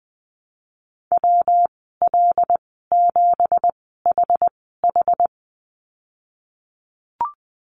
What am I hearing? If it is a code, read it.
Morse code, heard as a keyed tone, about 20 wpm, spaced as for 15 wpm, 700 Hz. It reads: WL7HH
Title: 179 Lesson 29g - Slash - Call Signs 20wpm